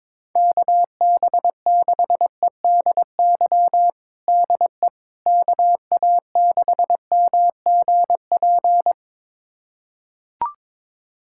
Morse code, heard as a keyed tone, about 22 wpm, 700 Hz. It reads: KB6EDY DE KA6MGP
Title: USA Callsign DE Callsign - 6 Characters - 22wpm